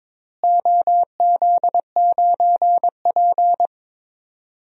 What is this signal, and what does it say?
Morse code, heard as a keyed tone, about 22 wpm, 700 Hz. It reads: OZ9P
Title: DXCC-100 Call Signs - 4 Characters - Repeated 3x - 22wpm